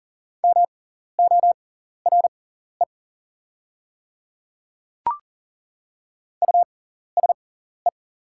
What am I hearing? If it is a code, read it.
Morse code, heard as a keyed tone, about 40 wpm, spaced as for 14 wpm, 690 Hz. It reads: MORE USE
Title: 6x Character Spacing - Top 100 Words - 40wpm